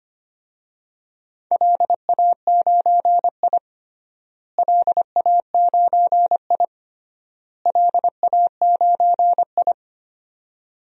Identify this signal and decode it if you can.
Morse code, heard as a keyed tone, about 25 wpm, 700 Hz. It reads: LA9I LA9I LA9I
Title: DXCC-100 Call Signs - 4 Characters - Repeated 3x - 25wpm